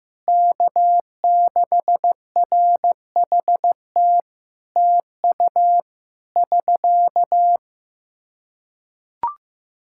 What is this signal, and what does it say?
Morse code, heard as a keyed tone, about 15 wpm, 700 Hz. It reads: K6RHT TU <SK>
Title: Practice QSO - Exchange 4 - Ending - 15wpm